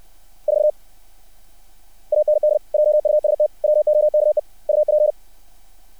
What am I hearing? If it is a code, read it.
Morse code, heard as a keyed tone, about 60 wpm, 600 Hz. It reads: 4 AND JUST WOKE UP
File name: archived / CRSnet